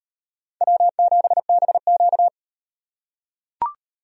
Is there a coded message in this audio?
Yes: Morse code from 0.6 to 2.3 s, heard as a keyed tone, about 38 wpm, 700 Hz. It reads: W7BQ